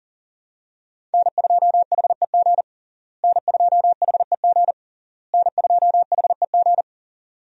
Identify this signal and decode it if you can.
Morse code, heard as a keyed tone, about 40 wpm, 700 Hz. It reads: N2HEG N2HEG N2HEG
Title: USA Call Signs - 5 Characters - Repeated 3x - 40wpm